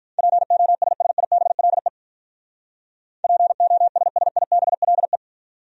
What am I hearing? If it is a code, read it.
Morse code, heard as a keyed tone, about 46 wpm, 700 Hz. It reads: POSSIBLE POSSIBLE